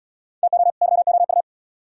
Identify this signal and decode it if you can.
Morse code, heard as a keyed tone, about 75 wpm, 700 Hz. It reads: TO YOU